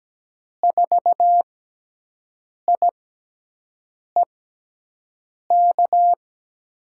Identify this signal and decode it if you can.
Morse code, heard as a keyed tone, about 17 wpm, spaced as for 6 wpm, 700 Hz. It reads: 4IEK
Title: USA Call Signs - 6 Characters - 6x Character Spacing - 17wpm